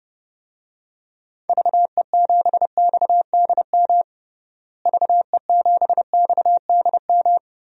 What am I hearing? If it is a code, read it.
Morse code, heard as a keyed tone, about 30 wpm, 700 Hz. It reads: VE7XDM VE7XDM
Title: DXCC-100 Call Signs - 6 Characters - Repeated 3x - 30wpm